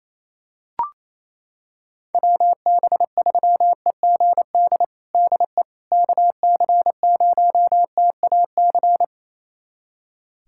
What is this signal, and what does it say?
Morse code, heard as a keyed tone, about 28 wpm, 700 Hz. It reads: WB3EGD DE KC0TAC
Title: USA Callsign DE Callsign - 6 Characters - 28wpm